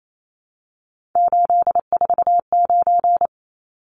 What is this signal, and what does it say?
Morse code, heard as a keyed tone, about 28 wpm, 700 Hz. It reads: 849
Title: Numbers - 3-Digit 28wpm